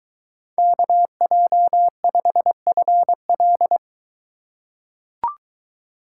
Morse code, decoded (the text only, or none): KJ5FL